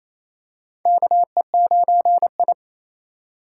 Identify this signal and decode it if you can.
Morse code, heard as a keyed tone, about 28 wpm, 700 Hz. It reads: KE9I
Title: K1USN SST QSO Practice - 28wpm